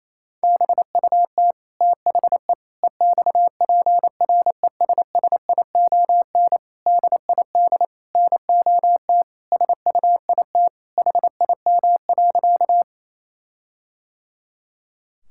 Morse code, heard as a keyed tone, about 27 wpm, 700 Hz. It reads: BUT THE EXPRESSION DID NOT SUIT HIM.
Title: Sentences from Top 1600 Words 28wpm